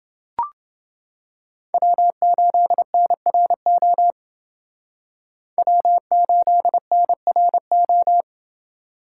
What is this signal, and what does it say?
Morse code, heard as a keyed tone, about 29 wpm, 700 Hz. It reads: W8NRO W8NRO